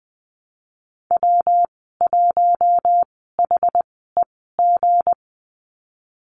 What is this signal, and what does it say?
Morse code, heard as a keyed tone, about 20 wpm, spaced as for 15 wpm, 700 Hz. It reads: W1HEG
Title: 173 Lesson 28g - Number 9 - Call Signs 20wpm